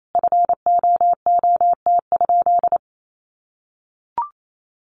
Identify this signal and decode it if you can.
Morse code, heard as a keyed tone, about 28 wpm, 700 Hz. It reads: FOOT?
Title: Sentences from Top 500 Words 28wpm